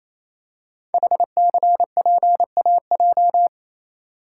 Morse code, heard as a keyed tone, about 27 wpm, 700 Hz. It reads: HCPAJ
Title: Code Groups - 5 Letters - 28wpm